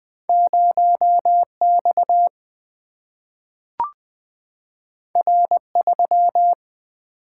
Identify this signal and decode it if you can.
Morse code, heard as a keyed tone, about 20 wpm, 700 Hz. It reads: 0X R3
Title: Single-Letter Single-Number 20wpm